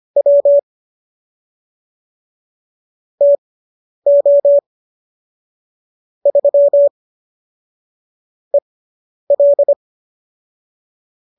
Morse code, heard as a keyed tone, about 25 wpm, spaced as for 10 wpm, 570 Hz. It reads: W TO 3 EL